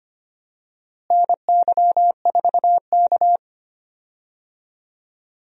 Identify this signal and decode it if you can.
Morse code, heard as a keyed tone, about 25 wpm, 700 Hz. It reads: NY4K
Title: USA Call Signs - 4 Characters - Repeated 2x - 25wpm